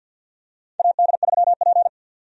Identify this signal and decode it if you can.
Morse code, heard as a keyed tone, about 52 wpm, 700 Hz. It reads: AD3P
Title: USA Call Signs - 4 Characters - 50wpm